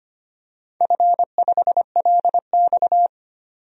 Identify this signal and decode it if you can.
Morse code, heard as a keyed tone, about 25 wpm, 700 Hz. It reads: F5LX
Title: DXCC-100 Call Signs - 4 Characters - 25wpm